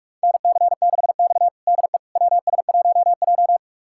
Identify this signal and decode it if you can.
Morse code, heard as a keyed tone, about 42 wpm, 700 Hz. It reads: NC6X DE WS1J
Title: USA Callsign DE Callsign - 4 Characters - 45wpm